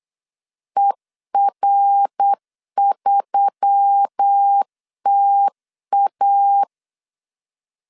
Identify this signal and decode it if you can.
Morse code, heard as a keyed tone, about 8 wpm, 800 Hz. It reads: ER3TA
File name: LEZIONI / 6 PKX0 / 6 PKX 8wpm 3041